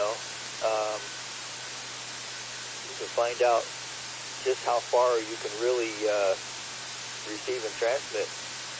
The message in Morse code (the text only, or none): none